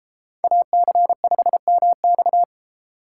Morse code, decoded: AC5MX

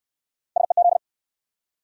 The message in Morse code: SEZ

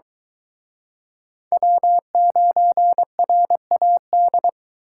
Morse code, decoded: W9RAD

